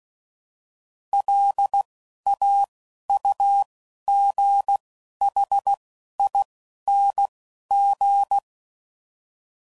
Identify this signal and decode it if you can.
Morse code, heard as a keyed tone, about 16 wpm, 800 Hz. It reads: L A U G H I N G
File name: Suffixes / Special INGs / Single INGs